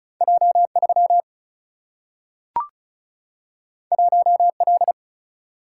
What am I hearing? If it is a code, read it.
Morse code, heard as a keyed tone, about 36 wpm, 700 Hz. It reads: J 3 1 L
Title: Single-Letter Single-Number 35wpm